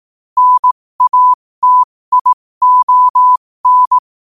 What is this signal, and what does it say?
Morse code, heard as a keyed tone, about 18 wpm, spaced as for 15 wpm, 1000 Hz. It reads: NATION
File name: qotdfiles / 15wpm